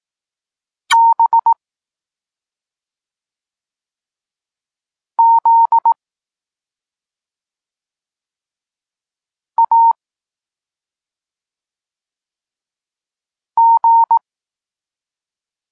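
Morse code, heard as a keyed tone, about 18 wpm, 920 Hz. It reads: B Z A G
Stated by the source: Polish Morse lesson teaching the letter C, from SP9MOA Niepołomicki Klub Krótkofalowców.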